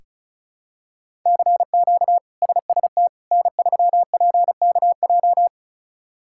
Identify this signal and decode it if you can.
Morse code, heard as a keyed tone, about 36 wpm, 700 Hz. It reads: CQ SST N3PKJ
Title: K1USN SST QSO Practice - 35wpm